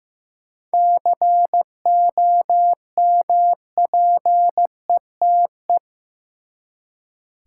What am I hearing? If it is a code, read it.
Morse code, heard as a keyed tone, about 15 wpm, 700 Hz. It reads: COMPETE
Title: Seven Letter Words - 15wpm